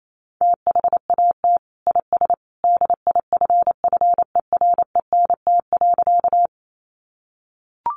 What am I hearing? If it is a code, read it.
Morse code, heard as a keyed tone, about 28 wpm, 700 Hz. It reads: THAT IS DIFFERENT.
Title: Sentences from Top 600 Words 28wpm